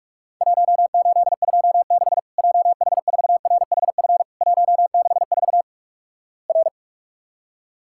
Morse code, heard as a keyed tone, about 46 wpm, 700 Hz. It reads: 1826 JH4RHF 164 R
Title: Worked All Europe DX Contest - QTC Exchange - 45wpm